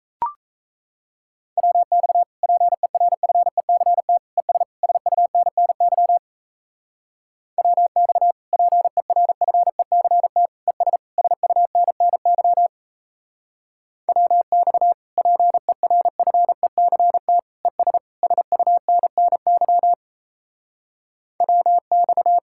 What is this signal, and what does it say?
Morse code, heard as a keyed tone, about 35 wpm, 700 Hz. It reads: WX PERFECT ES SUNNY WX PERFECT ES SUNNY WX PERFECT ES SUNNY WX
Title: Speed-Racing Common QSO Phrases Encore - 28wpm